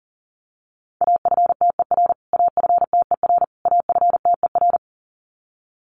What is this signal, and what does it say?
Morse code, heard as a keyed tone, about 37 wpm, 700 Hz. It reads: AFTER AFTER AFTER
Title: Top 200 Words Repeated 40wpm